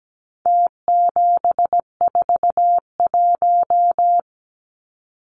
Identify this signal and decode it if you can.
Morse code, heard as a keyed tone, about 17 wpm, 700 Hz. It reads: T741
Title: Numbers - Contest Style - 4-Digit 17wpm